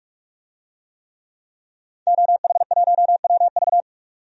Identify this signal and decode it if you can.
Morse code, heard as a keyed tone, about 45 wpm, 700 Hz. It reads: OH1WV
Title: DXCC-100 Call Signs - 5 Characters - 45wpm